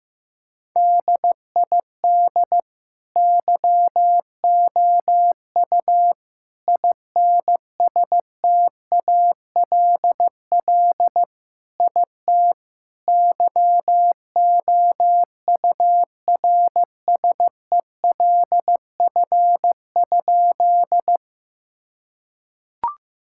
Morse code, heard as a keyed tone, about 15 wpm, 700 Hz. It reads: DID YOU INSTALL IT YOURSELF?
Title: Sentences from Top 1600 Words 15wpm